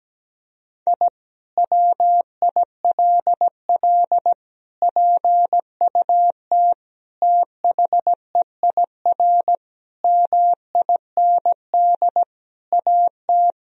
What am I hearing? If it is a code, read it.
Morse code, heard as a keyed tone, about 17 wpm, 700 Hz. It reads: I WILL PUT THEIR MIND AT
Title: Sentences from Top 1000 Words - Review Part 1 - 17wpm